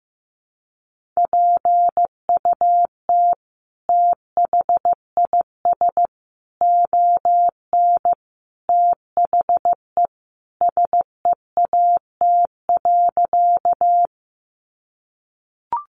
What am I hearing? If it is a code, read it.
Morse code, heard as a keyed tone, about 15 wpm, 700 Hz. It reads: PUT THIS ON THE SEAT.
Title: Sentences from Top 1500 Words 15wpm